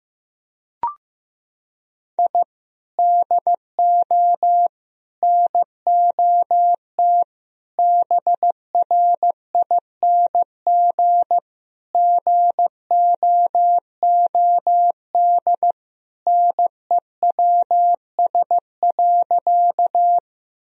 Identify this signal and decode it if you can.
Morse code, heard as a keyed tone, about 15 wpm, 700 Hz. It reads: I DO NOT BRING GOOD NEWS.
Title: Sentences from Top 1000 Words - Review Part 3 - 15wpm